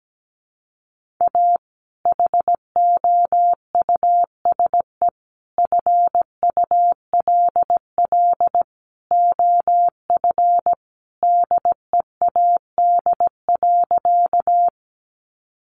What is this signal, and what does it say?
Morse code, heard as a keyed tone, about 17 wpm, 700 Hz. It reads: A HOUSE FULL OF DEAD.
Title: Sentences from Top 1500 Words 17wpm